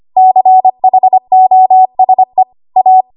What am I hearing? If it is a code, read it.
Morse code, heard as a keyed tone, about 25 wpm, 750 Hz. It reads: CHOSE A
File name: Archive / 25 WPM / 25WPM